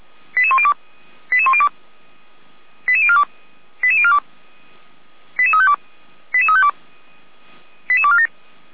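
Five-tone selective call (ZVEI), heard as quick runs of five tones, tones of about 70 ms. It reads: 88102 88102 88042 88042 88352 88352 88247
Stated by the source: Five-tone ZVEI selective call received on SDR.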